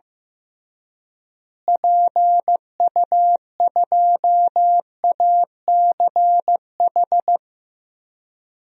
Morse code, heard as a keyed tone, about 16 wpm, 700 Hz. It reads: PU2ACH
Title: DXCC-100 Call Signs - 6 Characters - Repeated 3x - 15wpm